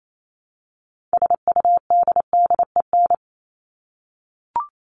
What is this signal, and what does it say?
Morse code, heard as a keyed tone, about 28 wpm, 700 Hz. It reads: SUDDEN
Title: LY Suffix - Top 100 Words with Base Words 28wpm